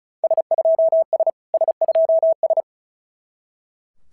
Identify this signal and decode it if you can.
Morse code, heard as a keyed tone, about 36 wpm, 660 Hz. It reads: S2S S2S
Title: SOTA QSO Practice 35wpm